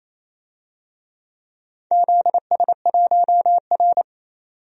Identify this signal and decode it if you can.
Morse code, heard as a keyed tone, about 27 wpm, 700 Hz. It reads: ZS1R
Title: DXCC-100 Call Signs - 4 Characters - 28wpm